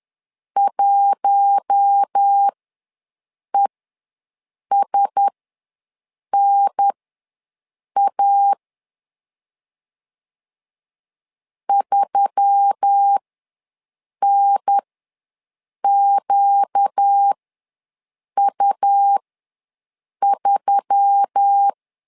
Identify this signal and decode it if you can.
Morse code, heard as a keyed tone, about 10 wpm, spaced as for 6 wpm, 800 Hz. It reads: 1ESNA 3NQU3
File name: LEZIONI / 4 Q1S3 ws 10wpm 2437